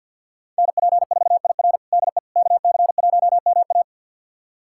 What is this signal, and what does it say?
Morse code, heard as a keyed tone, about 52 wpm, 690 Hz. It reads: NP4IR DE KC1MA
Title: USA Callsign DE Callsign - 5 Characters - 50wpm